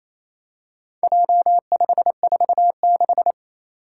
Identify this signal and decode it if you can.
Morse code, heard as a keyed tone, about 27 wpm, 700 Hz. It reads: J546
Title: Four Challenging Letter-Number Combinations 28wpm